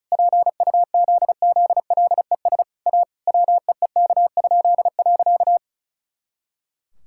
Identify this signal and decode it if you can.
Morse code, heard as a keyed tone, about 34 wpm, 700 Hz. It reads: PUZZLES A WEEK?.
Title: Pangram Challenge v2 - 35wpm